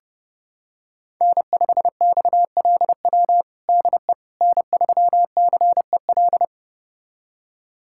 Morse code, heard as a keyed tone, about 32 wpm, 700 Hz. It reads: N5XLW DE N3CEL
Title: USA Callsign DE Callsign 30wpm